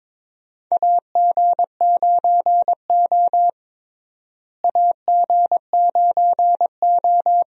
Morse code, heard as a keyed tone, about 22 wpm, 700 Hz. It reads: AG9O AG9O
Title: USA Call Signs - 4 Characters - Repeated 2x - 22wpm